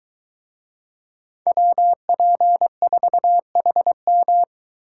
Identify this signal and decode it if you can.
Morse code, heard as a keyed tone, about 23 wpm, 700 Hz. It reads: WP4HM